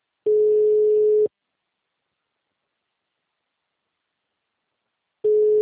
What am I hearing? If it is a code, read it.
Ringback tone (Europe, 425 Hz).